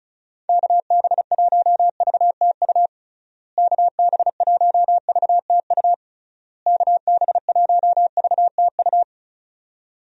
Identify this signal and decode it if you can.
Morse code, heard as a keyed tone, about 36 wpm, 700 Hz. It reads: KB1VTU KB1VTU KB1VTU
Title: USA Call Signs - 6 Characters - Repeated 3x - 35wpm